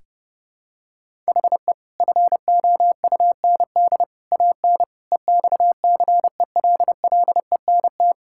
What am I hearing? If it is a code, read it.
Morse code, heard as a keyed tone, about 30 wpm, 700 Hz. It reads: HE FOUND AN EXCELLENT
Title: Sentences from Top 1000 Words - Review Part 6 - 30wpm